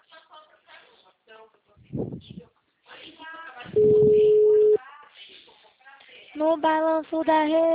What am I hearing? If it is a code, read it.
Ringback tone (Europe, 425 Hz), with speech.
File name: out- -2013- - - .29820